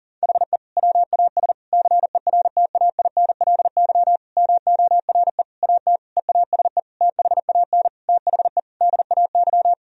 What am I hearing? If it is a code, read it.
Morse code, heard as a keyed tone, about 40 wpm, 700 Hz. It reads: HE WAS CERTAINLY MORE AT EASE THAN THE DAY
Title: Sentences from Top 1000 Words - Review Part 1 - 40wpm